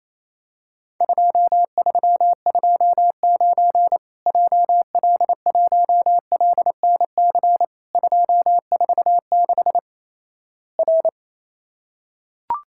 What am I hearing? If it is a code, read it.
Morse code, heard as a keyed tone, about 28 wpm, 700 Hz. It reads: 2329 JL1LNC 246 R
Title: Worked All Europe DX Contest - QTC Exchange - 28wpm